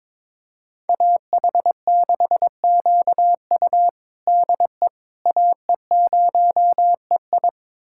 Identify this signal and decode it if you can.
Morse code, heard as a keyed tone, about 22 wpm, 700 Hz. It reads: AH6QU DE AE0EI
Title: USA Callsign DE Callsign - 5 Characters - 22wpm